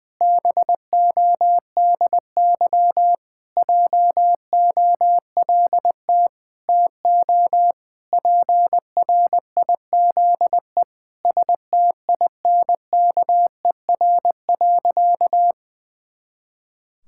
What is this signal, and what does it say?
Morse code, heard as a keyed tone, about 20 wpm, 700 Hz. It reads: BODY JOLT TO PRIZE STINKER.
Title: Pangram Challenge v2 - 20wpm